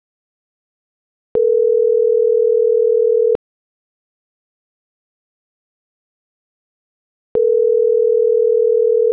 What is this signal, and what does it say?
Ringback tone (North America), 440 + 480 Hz.